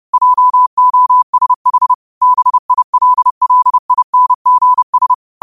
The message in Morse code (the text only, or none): JOSH BILLINGS